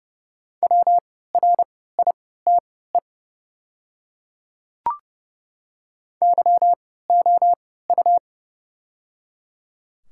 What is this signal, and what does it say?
Morse code, heard as a keyed tone, about 30 wpm, spaced as for 17 wpm, 700 Hz. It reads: WRITE YOU
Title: 3x Character Spacing - Top 300 Words - 30wpm